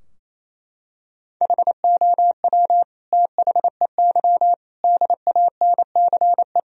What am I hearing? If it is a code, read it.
Morse code, heard as a keyed tone, about 28 wpm, 700 Hz. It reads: HOW THEY DANCE
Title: Sentences from Top 1000 Words - Review Part 4 - 28wpm